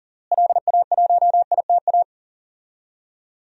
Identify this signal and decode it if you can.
Morse code, heard as a keyed tone, about 40 wpm, 700 Hz. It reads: LA1ITA
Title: DXCC-100 Call Signs - 6 Characters - Repeated 3x - 40wpm